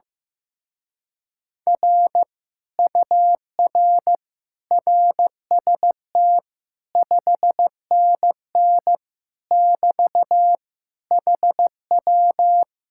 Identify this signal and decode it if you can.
Morse code, heard as a keyed tone, about 15 wpm, 700 Hz. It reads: R UR RST 5NN <BT> HW
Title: Speed-Racing Common QSO Phrases Encore - 15wpm